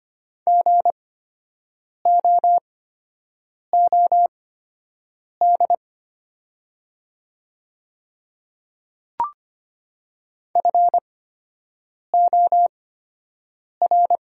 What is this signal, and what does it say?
Morse code, heard as a keyed tone, about 25 wpm, spaced as for 7 wpm, 700 Hz. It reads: GOOD FOR
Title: 8x Character Spacing - Top 200 Words - 25wpm